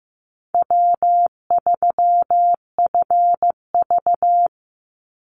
Morse code, heard as a keyed tone, about 15 wpm, 700 Hz. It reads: W3FV
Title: Easy Call Signs - 15wpm